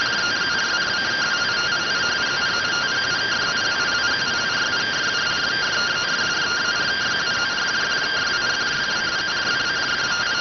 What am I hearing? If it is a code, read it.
Two-tone teleprinter signal, about 170 Hz shift.